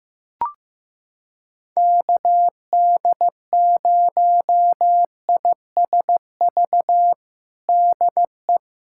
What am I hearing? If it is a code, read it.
Morse code, heard as a keyed tone, about 15 wpm, 700 Hz. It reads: KD0ISV DE
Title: USA Callsign DE Callsign 15wpm